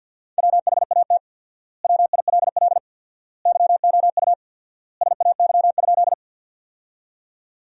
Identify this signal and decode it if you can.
Morse code, heard as a keyed tone, about 50 wpm, 700 Hz. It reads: WHAT WILL YOU SAY?
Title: 4x Word-Spacing - Sentences from Top 200 Words - 50wpm